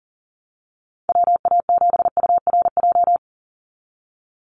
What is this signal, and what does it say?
Morse code, heard as a keyed tone, about 40 wpm, 700 Hz. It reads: WA7URJ
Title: USA Call Signs 40wpm